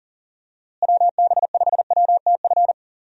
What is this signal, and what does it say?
Morse code, heard as a keyed tone, about 40 wpm, 700 Hz. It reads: WB5WTF
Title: USA Call Signs - 6 Characters - Repeated 2x - 40wpm